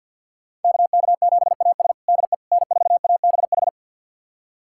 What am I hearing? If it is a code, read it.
Morse code, heard as a keyed tone, about 48 wpm, 700 Hz. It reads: KK7AS DE N4ABH
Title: USA Callsign DE Callsign - 5 Characters - 50wpm